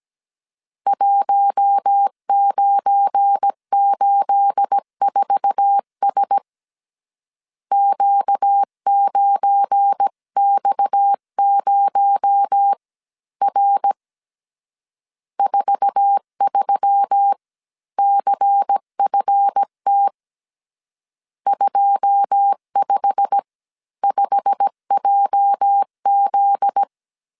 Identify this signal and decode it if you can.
Morse code, heard as a keyed tone, about 17 wpm, spaced as for 14 wpm, 800 Hz. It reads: 1984S Q9X0R 43CFT 255JZ
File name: CW / DET / 1 9 1500C S C 16wpm 2909